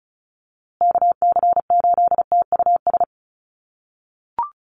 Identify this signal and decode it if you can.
Morse code, heard as a keyed tone, about 34 wpm, 700 Hz. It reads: KC8TUS
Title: USA Call Signs Repeated 2x 35wpm